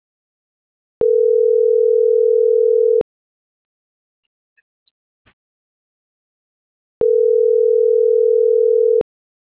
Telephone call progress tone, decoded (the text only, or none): ringback tone